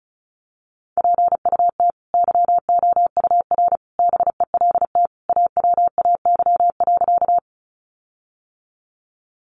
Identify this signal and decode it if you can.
Morse code, heard as a keyed tone, about 34 wpm, 700 Hz. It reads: PUT YOUR BELT AWAY.
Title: Sentences from Top 1400 Words 35wpm